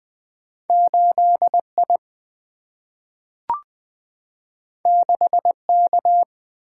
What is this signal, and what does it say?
Morse code, heard as a keyed tone, about 20 wpm, 700 Hz. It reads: 8I 6K
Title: Two Letter-Number Combinations - 20wpm